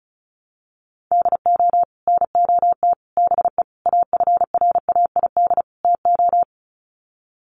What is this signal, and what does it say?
Morse code, heard as a keyed tone, about 34 wpm, 700 Hz. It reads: DO NOT BE AFRAID TO
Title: Sets of 5 Words Encore2 35wpm